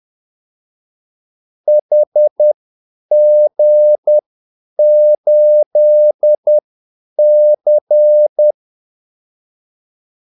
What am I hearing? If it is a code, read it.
Morse code, heard as a keyed tone, about 10 wpm, spaced as for 8 wpm, 600 Hz. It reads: HG8C
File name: Callsigns-10-8wpm